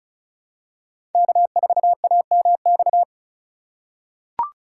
Morse code, heard as a keyed tone, about 36 wpm, 700 Hz. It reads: K4AMX
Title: USA Call Signs - 5 Characters - 3x Character Spacing - 35wpm